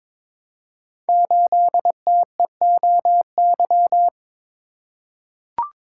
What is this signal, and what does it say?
Morse code, heard as a keyed tone, about 23 wpm, 700 Hz. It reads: 8TEOY
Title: Code Groups - 5 Letter:Numbers - 22wpm